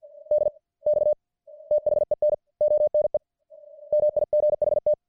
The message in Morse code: DX THEN ONE MIGHT